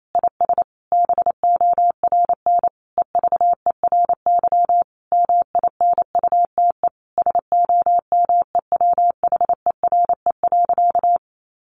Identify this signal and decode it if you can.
Morse code, heard as a keyed tone, about 28 wpm, 700 Hz. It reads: IS BORN EVERY MINUTE SOMEWHERE.
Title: Sentences from Top 1400 Words 28wpm